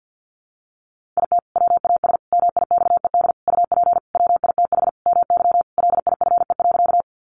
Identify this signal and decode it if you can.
Morse code, heard as a keyed tone, about 50 wpm, 700 Hz. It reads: IT WAS MIXED UP WITH MY LIFE.
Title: Sentences from Top 1500 Words 50wpm